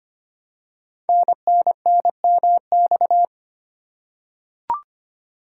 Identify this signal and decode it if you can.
Morse code, heard as a keyed tone, about 25 wpm, 700 Hz. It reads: NNNMX
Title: Code Groups - 5 Letters - 25wpm